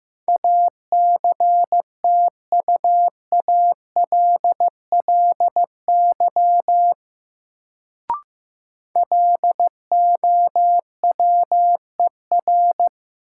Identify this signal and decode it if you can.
Morse code, heard as a keyed tone, about 15 wpm, 700 Hz. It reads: ACTUALLY LOWER
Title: Sentences from Top 700 Words - New Word Drill - 15wpm